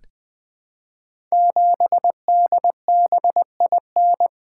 Morse code, heard as a keyed tone, about 20 wpm, 700 Hz. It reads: 7DBIN
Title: Code Groups - 5 Letter:Numbers - 20wpm